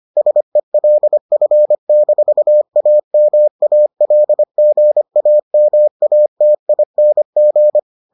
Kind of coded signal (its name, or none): Morse code